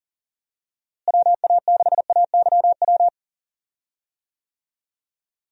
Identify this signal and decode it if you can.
Morse code, heard as a keyed tone, about 40 wpm, 700 Hz. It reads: WA6AYW
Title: USA Call Signs - 6 Characters - Repeated 2x - 40wpm